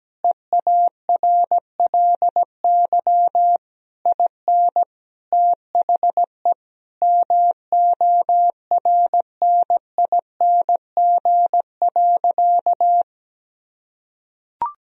Morse code, heard as a keyed tone, about 17 wpm, 700 Hz. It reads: EARLY IN THE MORNING.
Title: Sentences from Top 1000 Words - Review Part 2 - 17wpm